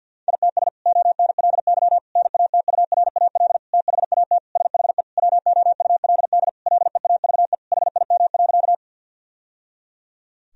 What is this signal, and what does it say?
Morse code, heard as a keyed tone, about 50 wpm, 700 Hz. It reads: ITS ONLY NATURAL THAT SHE WOULD LEAVE HIM.
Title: Sentences from Top 1000 Words - Review Part 4 - 50wpm